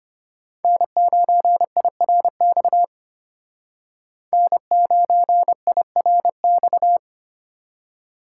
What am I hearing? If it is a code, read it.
Morse code, heard as a keyed tone, about 32 wpm, 700 Hz. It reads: N9IRX N9IRX